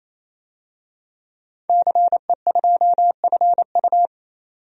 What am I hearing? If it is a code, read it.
Morse code, heard as a keyed tone, about 27 wpm, 700 Hz. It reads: CE2FU
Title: DXCC-100 Call Signs - 5 Characters - 28wpm